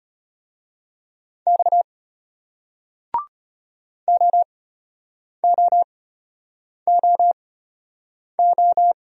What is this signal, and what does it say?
Morse code, heard as a keyed tone, about 35 wpm, spaced as for 8 wpm, 700 Hz. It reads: X OOOO